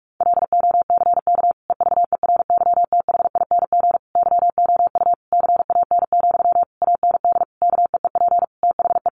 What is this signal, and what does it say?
Morse code, heard as a keyed tone, about 44 wpm, 700 Hz. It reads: LOCK EVERYTHING YOU CAN, AND KEEP THE
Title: Sentences from Top 1300 Words 45wpm